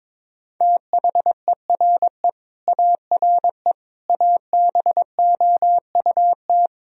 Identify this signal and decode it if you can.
Morse code, heard as a keyed tone, about 22 wpm, 700 Hz. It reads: THERE ARE ABOUT
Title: Sentences from Top 1000 Words - Review Part 2 - 22wpm